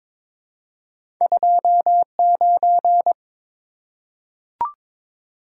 Morse code, heard as a keyed tone, about 22 wpm, 700 Hz. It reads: 29
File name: practice / 22wpm / Two Letter-Number Combinations - 22wpm